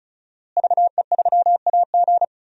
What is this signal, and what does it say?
Morse code, heard as a keyed tone, about 36 wpm, 700 Hz. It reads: VE3AG
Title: DXCC-100 Call Signs - 5 Characters - Repeated 2x - 35wpm